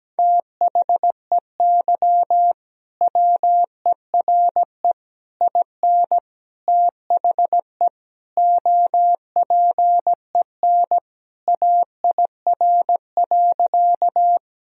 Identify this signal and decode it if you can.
Morse code, heard as a keyed tone, about 17 wpm, 700 Hz. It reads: THEY WERE IN THE OPEN AIR.
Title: Sentences from Top 1000 Words - Review Part 1 - 17wpm